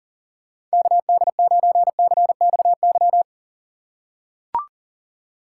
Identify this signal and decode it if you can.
Morse code, heard as a keyed tone, about 40 wpm, 700 Hz. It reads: KD9CXY
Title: USA Call Signs - 2x Character Spacing - 40wpm